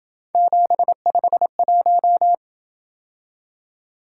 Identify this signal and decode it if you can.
Morse code, heard as a keyed tone, about 28 wpm, 700 Hz. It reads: 751